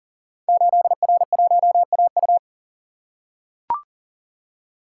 Morse code, heard as a keyed tone, about 38 wpm, 700 Hz. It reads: 8R1AU
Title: Code Groups - 5 Letter:Numbers - 40wpm